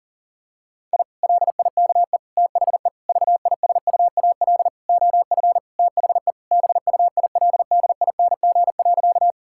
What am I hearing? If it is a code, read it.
Morse code, heard as a keyed tone, about 40 wpm, 700 Hz. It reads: I LIKE THE VISUAL OF THE BUILDING.
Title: Sentences from Top 1000 Words - Review Part 6 - 40wpm